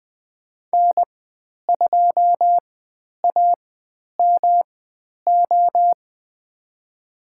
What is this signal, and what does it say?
Morse code, heard as a keyed tone, about 20 wpm, spaced as for 10 wpm, 700 Hz. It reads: N2AMO
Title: 109 Lesson 18g - Letter m - Call Signs 20wpm